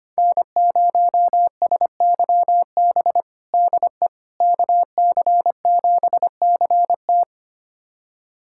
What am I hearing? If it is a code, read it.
Morse code, heard as a keyed tone, about 25 wpm, 700 Hz. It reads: N0SYB DE KC7CT
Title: USA Callsign DE Callsign - 5 Characters - 25wpm